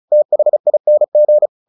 Morse code, heard as a keyed tone, about 34 wpm, 600 Hz. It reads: THING
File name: the old man and the sea / the old man and the sea 1 35wpm